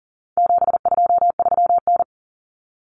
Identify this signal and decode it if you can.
Morse code, heard as a keyed tone, about 40 wpm, 700 Hz. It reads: 723N
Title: Common Cut Numbers - 4-Digit 40wpm